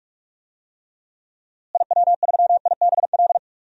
Morse code, heard as a keyed tone, about 46 wpm, 700 Hz. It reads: IW3IBL
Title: DXCC-100 Call Signs - 6 Characters - Repeated 3x - 45wpm